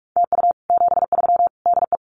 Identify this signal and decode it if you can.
Morse code, heard as a keyed tone, about 44 wpm, 700 Hz. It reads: TU 73 DE
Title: Common QSO Phrases Encore 45wpm